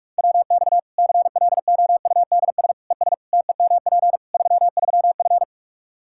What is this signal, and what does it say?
Morse code, heard as a keyed tone, about 46 wpm, 700 Hz. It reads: WX CLOUDS ES TEMP 33F
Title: K1USN SST QSO Practice 45wpm